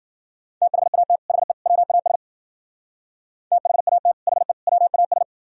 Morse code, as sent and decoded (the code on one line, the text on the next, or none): - .... .- - / .... / .-- .- ... / - .... .- - / .... / .-- .- ...
THAT HE WAS THAT HE WAS